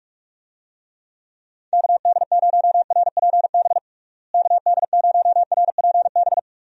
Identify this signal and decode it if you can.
Morse code, heard as a keyed tone, about 46 wpm, 700 Hz. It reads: KD0RPB KD0RPB
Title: USA Call Signs - 6 Characters - Repeated 3x - 45wpm